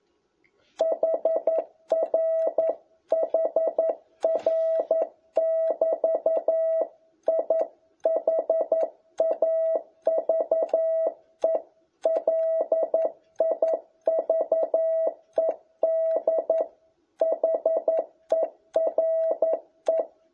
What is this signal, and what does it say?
Morse code, heard as a keyed tone, about 11 wpm, 650 Hz. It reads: HR HR<BT> I HAVE LIVED HERE
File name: jh8cbh / 2021cw